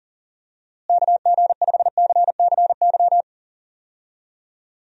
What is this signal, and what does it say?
Morse code, heard as a keyed tone, about 37 wpm, 700 Hz. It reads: KG5CCY
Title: USA Call Signs - 6 Characters - Repeated 2x - 40wpm